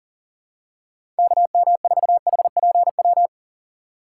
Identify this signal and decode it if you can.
Morse code, heard as a keyed tone, about 40 wpm, 700 Hz. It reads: KM4HPW
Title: USA Call Signs - 6 Characters - 2x Character Spacing - 40wpm